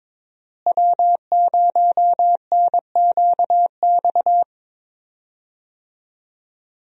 Morse code, heard as a keyed tone, about 22 wpm, 700 Hz. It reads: W0NQX
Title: USA Call Signs - 5 Characters - Repeated 2x - 22wpm